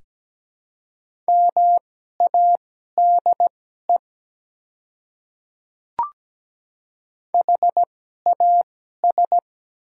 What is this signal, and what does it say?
Morse code, heard as a keyed tone, about 17 wpm, spaced as for 12 wpm, 700 Hz. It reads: MADE HAS
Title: 2x Character Spacing - Top 300 Words - 17wpm